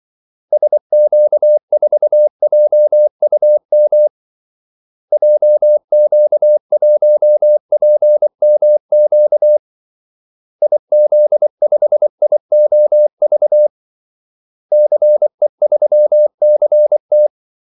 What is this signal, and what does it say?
Morse code, heard as a keyed tone, about 24 wpm, 600 Hz. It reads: SQ4JUM JQ1PMQ IZ5IOV CE3CT
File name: Callsigns-24wpm